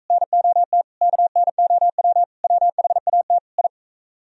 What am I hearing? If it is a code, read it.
Morse code, heard as a keyed tone, about 42 wpm, 700 Hz. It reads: NOT KNOW WHAT I